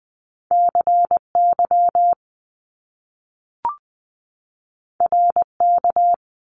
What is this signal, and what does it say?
Morse code, heard as a keyed tone, about 20 wpm, 700 Hz. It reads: CY RK
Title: Top Two Letter Combinations in Words 20wpm